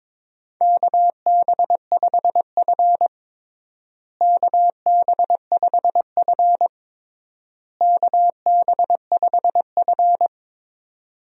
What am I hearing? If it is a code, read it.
Morse code, heard as a keyed tone, about 22 wpm, 700 Hz. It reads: KB5F KB5F KB5F
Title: USA Call Signs - 4 Characters - Repeated 3x - 22wpm